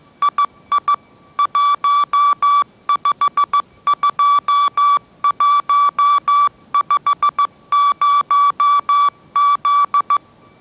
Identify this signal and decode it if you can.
Morse code, heard as a keyed tone, about 15 wpm, 1150 Hz. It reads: II 152150Z
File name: mrx / mrx065 16wpm